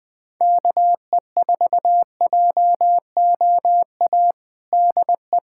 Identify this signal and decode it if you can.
Morse code, heard as a keyed tone, about 20 wpm, 700 Hz. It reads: KE4JOA DE
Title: USA Callsign DE Callsign - 6 Characters - 20wpm